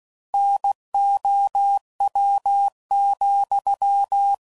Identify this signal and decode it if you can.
Morse code, heard as a keyed tone, about 16 wpm, 800 Hz. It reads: NOW,
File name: CW Code Groups @15WPM / Technical Subjects / OR Gate